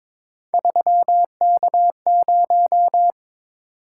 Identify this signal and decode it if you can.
Morse code, heard as a keyed tone, about 22 wpm, 700 Hz. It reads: 3K0
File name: practice / 22wpm / Three Letter-Number Combinations - 22wpm